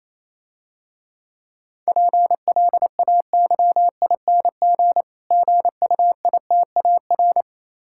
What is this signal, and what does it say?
Morse code, heard as a keyed tone, about 29 wpm, 700 Hz. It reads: PLAYING GUITAR